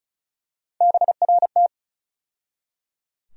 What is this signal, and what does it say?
Morse code, heard as a keyed tone, about 34 wpm, 700 Hz. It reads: BRT
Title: USA Call Sign Postfixes - 3 Letters 35wpm